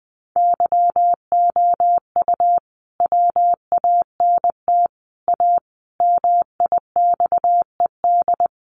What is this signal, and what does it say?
Morse code, heard as a keyed tone, about 20 wpm, 700 Hz. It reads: YOU WANT A MIXED
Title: Sentences from Top 1500 Words 20wpm